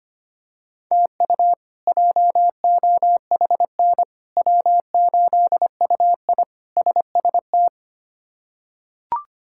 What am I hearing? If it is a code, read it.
Morse code, heard as a keyed tone, about 25 wpm, 700 Hz. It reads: TU JOHN W8UI SST
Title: K1USN SST QSO Practice - 25wpm